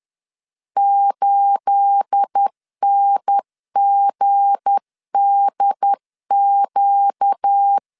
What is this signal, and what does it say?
Morse code, heard as a keyed tone, about 10 wpm, 800 Hz. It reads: 8NGDQ
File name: LEZIONI / 9 28 10wpm 2842